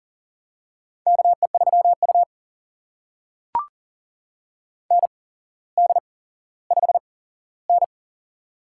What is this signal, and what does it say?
Morse code, heard as a keyed tone, about 40 wpm, spaced as for 17 wpm, 700 Hz. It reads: KE3U ND5N
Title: USA Call Signs - 4 Characters - 8x Character Spacing - 40wpm